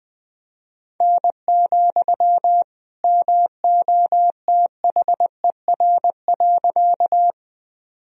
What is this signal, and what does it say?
Morse code, heard as a keyed tone, about 21 wpm, 700 Hz. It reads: N, MOTHER.
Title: Sentences from Top 200 Words and Abbreviations 20wpm